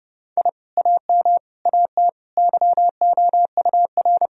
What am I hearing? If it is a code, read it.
Morse code, heard as a keyed tone, about 32 wpm, 700 Hz. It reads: I AM AT YOUR
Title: Sentences from Top 1000 Words - Review Part 4 - 30wpm